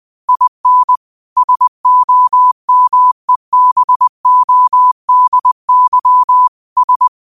Morse code, heard as a keyed tone, about 20 wpm, 1000 Hz. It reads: IN SOMEBODY S